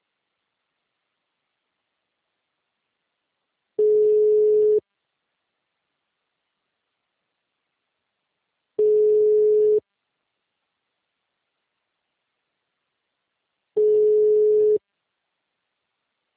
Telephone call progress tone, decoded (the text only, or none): ringback tone